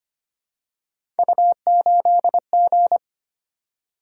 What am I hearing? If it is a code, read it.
Morse code, heard as a keyed tone, about 25 wpm, 700 Hz. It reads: U8G
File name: Three Letter-Number Combinations - 25wpm